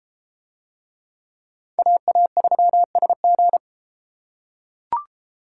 Morse code, heard as a keyed tone, about 31 wpm, 700 Hz. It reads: AA3SG